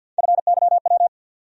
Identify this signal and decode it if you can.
Morse code, heard as a keyed tone, about 49 wpm, 700 Hz. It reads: FYW
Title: USA Call Sign Postfixes - 3 Letters 50wpm